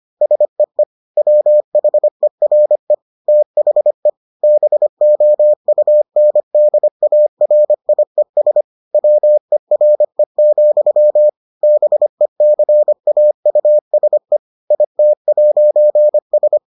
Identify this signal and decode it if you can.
Morse code, heard as a keyed tone, about 25 wpm, 600 Hz. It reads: SEE WHERE THE BOUNDARIES WERE, BECAUSE IT'S